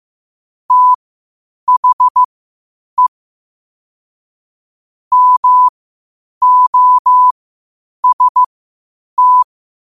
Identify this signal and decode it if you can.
Morse code, heard as a keyed tone, about 15 wpm, spaced as for 8 wpm, 1000 Hz. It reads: THE MOST
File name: qotdfiles / 8wpm